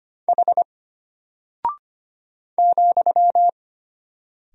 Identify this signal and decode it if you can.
Morse code, heard as a keyed tone, about 25 wpm, 700 Hz. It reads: H ,
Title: Single Character (Letter:Number:Punctuation:Prosign) Alt 25wpm